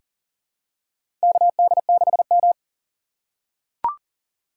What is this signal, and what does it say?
Morse code, heard as a keyed tone, about 40 wpm, 700 Hz. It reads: KD6M
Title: USA Call Signs - 4 Characters - 2x Character Spacing - 40wpm